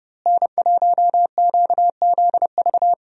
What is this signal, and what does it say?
Morse code, heard as a keyed tone, about 30 wpm, 700 Hz. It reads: N1QZV
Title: USA Call Signs - 5 Characters - Repeated 2x - 30wpm